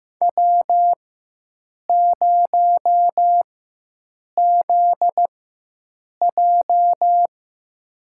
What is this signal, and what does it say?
Morse code, heard as a keyed tone, about 15 wpm, spaced as for 7 wpm, 700 Hz. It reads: W0ZJ
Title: USA Call Signs - 4x Character Spacing - 15wpm